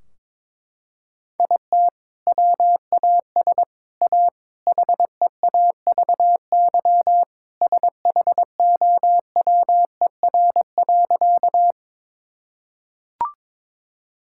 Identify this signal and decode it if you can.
Morse code, heard as a keyed tone, about 22 wpm, 700 Hz. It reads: IT WAS A HEAVY SHOWER.
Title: Sentences from Top 1600 Words 22wpm